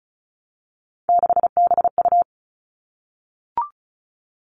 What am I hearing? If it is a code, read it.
Morse code, heard as a keyed tone, about 34 wpm, 700 Hz. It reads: 6BU
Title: All Cut Numbers - 3-Digit 35wpm